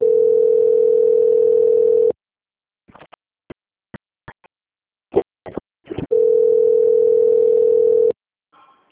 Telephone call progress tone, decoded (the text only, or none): ringback tone